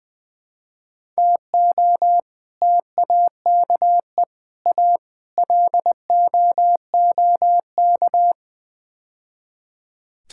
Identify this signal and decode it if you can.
Morse code, heard as a keyed tone, about 20 wpm, 700 Hz. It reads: TO TAKE A LOOK
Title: Build A Phrase - 5-Word Phrases - 20wpm